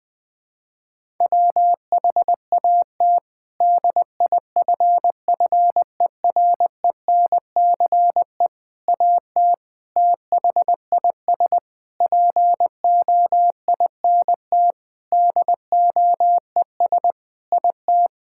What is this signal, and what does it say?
Morse code, heard as a keyed tone, about 21 wpm, 700 Hz. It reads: WHAT DIFFERENCE AT THIS POINT DOES IT
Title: Sentences with 8-Word Phrases - Spoken Beginning - 20wpm